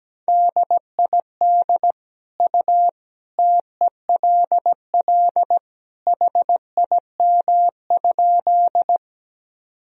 Morse code, heard as a keyed tone, about 17 wpm, 700 Hz. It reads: DID U TELL HIM?
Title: Sentences from Top 200 Words and Abbreviations 17wpm